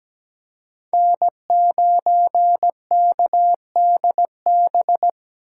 Morse code, heard as a keyed tone, about 17 wpm, 700 Hz. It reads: N9KDB
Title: K1USN SST QSO Practice - 17wpm